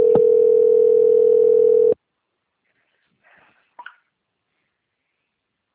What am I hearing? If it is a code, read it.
Ringback tone (North America), 440 + 480 Hz.
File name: out- - - - - .83186